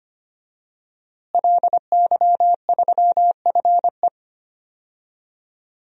Morse code, heard as a keyed tone, about 25 wpm, 700 Hz. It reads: LY3FE
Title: DXCC-100 Call Signs - 5 Characters - Repeated 2x - 25wpm